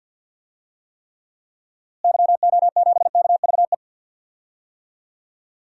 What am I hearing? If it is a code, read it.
Morse code, heard as a keyed tone, about 47 wpm, 700 Hz. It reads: YO7KVE
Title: DXCC-100 Call Signs - 6 Characters - Repeated 3x - 50wpm